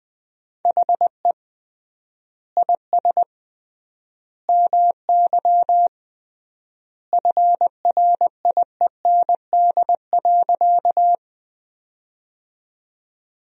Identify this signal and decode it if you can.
Morse code, heard as a keyed tone, about 20 wpm, 700 Hz. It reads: HE IS MY FRIEND.
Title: 3x Word-Spacing - Sentences from Top 300 Words - 20wpm